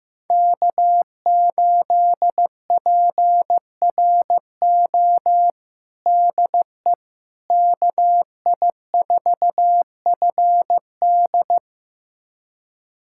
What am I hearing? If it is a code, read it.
Morse code, heard as a keyed tone, about 15 wpm, 700 Hz. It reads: K8PRO DE KI4FD
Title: USA Callsign DE Callsign - 5 Characters - 15wpm